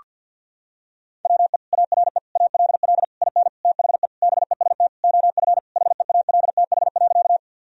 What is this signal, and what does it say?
Morse code, heard as a keyed tone, about 53 wpm, 700 Hz. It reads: WE ARE ALL IN THE BEST OF HEALTH.
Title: Sentences from Top 1000 Words - Review Part 3 - 50wpm